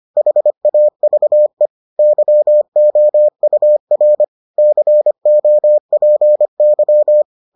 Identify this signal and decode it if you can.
Morse code, heard as a keyed tone, about 25 wpm, 600 Hz. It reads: HAVE YOUR COPY